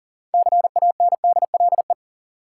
Morse code, heard as a keyed tone, about 40 wpm, 700 Hz. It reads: CANDLE
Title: Six Letter Words - 40wpm